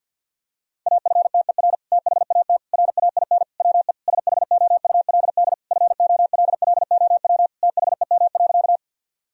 Morse code, heard as a keyed tone, about 49 wpm, 700 Hz. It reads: AFTER THAT RAIN WE SHOULD FOLLOW THEM.
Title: Sentences from Top 1000 Words - Review Part 1 - 50wpm